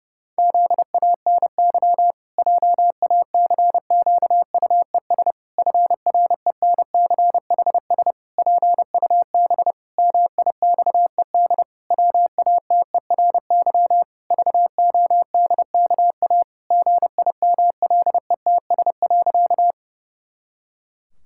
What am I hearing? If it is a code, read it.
Morse code, heard as a keyed tone, about 30 wpm, 700 Hz. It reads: ZANY JACQUES FRENCHS PUB MIXED WATERY VODKA GIMLETS.
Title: Pangram Challenge v2 - 30wpm